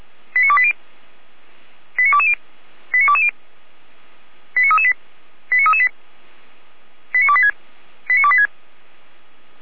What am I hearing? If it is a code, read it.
Five-tone selective call (ZVEI), heard as quick runs of five tones, tones of about 80 ms. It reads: 89280 89220 89220 89338 89338 89286 89286